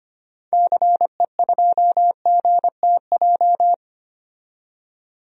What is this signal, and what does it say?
Morse code, heard as a keyed tone, about 25 wpm, 700 Hz. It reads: CE2GTJ
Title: DXCC-100 Call Signs - 6 Characters - 25wpm